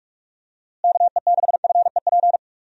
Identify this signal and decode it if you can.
Morse code, heard as a keyed tone, about 46 wpm, 700 Hz. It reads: KE6FEP
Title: USA Call Signs - 6 Characters - Repeated 2x - 45wpm